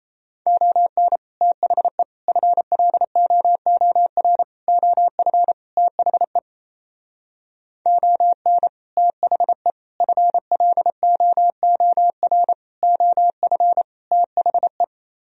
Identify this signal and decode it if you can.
Morse code, heard as a keyed tone, about 30 wpm, 700 Hz. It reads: ON THE FLOOR OF THE ON THE FLOOR OF THE
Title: Speed-Racing - Sets of 5 Words 28wpm